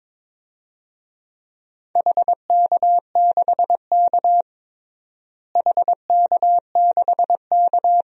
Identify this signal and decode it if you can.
Morse code, heard as a keyed tone, about 22 wpm, 700 Hz. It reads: HK6K HK6K
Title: DXCC-100 Call Signs - 4 Characters - Repeated 2x - 22wpm